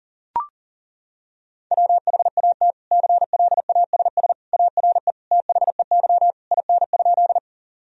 Morse code, heard as a keyed tone, about 40 wpm, 700 Hz. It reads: WHAT CLASS ARE THEY IN?
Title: Sentences from Top 1000 Words - Review Part 2 - 40wpm